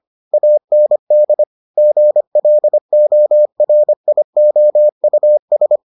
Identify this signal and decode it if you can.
Morse code, heard as a keyed tone, about 25 wpm, 600 Hz. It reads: AND GLORIOUS